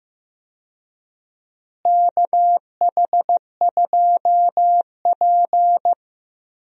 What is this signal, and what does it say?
Morse code, heard as a keyed tone, about 15 wpm, 700 Hz. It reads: KH2P
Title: USA Call Signs - 4 Characters - Repeated 2x - 15wpm